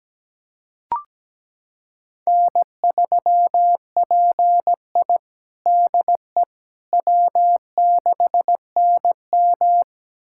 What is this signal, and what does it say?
Morse code, heard as a keyed tone, about 17 wpm, 700 Hz. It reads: N3PI DE W6NM
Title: USA Callsign DE Callsign - 4 Characters - 17wpm